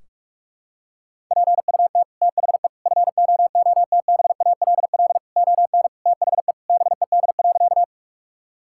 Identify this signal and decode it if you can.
Morse code, heard as a keyed tone, about 46 wpm, 700 Hz. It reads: PUT THE FOOTBALL ON THE BED.
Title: Sentences from Top 1600 Words 45wpm